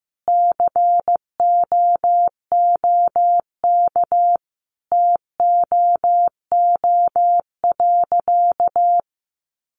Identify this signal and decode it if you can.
Morse code, heard as a keyed tone, about 15 wpm, 700 Hz. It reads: COOK TOO.
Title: Sentences from Top 700 Words 15wpm